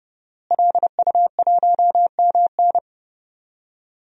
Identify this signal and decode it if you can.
Morse code, heard as a keyed tone, about 30 wpm, 700 Hz. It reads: LU1MN
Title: DXCC-100 Call Signs - 5 Characters - Repeated 2x - 30wpm